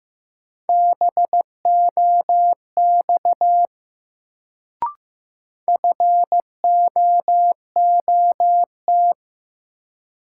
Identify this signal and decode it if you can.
Morse code, heard as a keyed tone, about 15 wpm, 700 Hz. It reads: BOX FOOT
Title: Sentences from Top 500 Words - New Word Drill - 15wpm